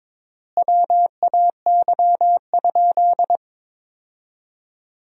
Morse code, heard as a keyed tone, about 21 wpm, 700 Hz. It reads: WAY?
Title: 3x Word-Spacing - Sentences from Top 200 Words - 22wpm